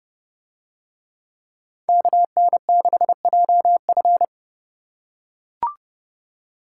Morse code, heard as a keyed tone, about 29 wpm, 700 Hz. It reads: KN6JF